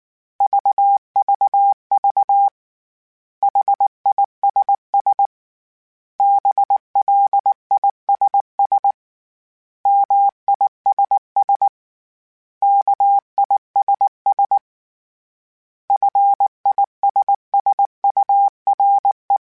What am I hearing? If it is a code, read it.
Morse code, heard as a keyed tone, about 19 wpm, 800 Hz. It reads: VVV HISS BLISS MISS KISS FISSURE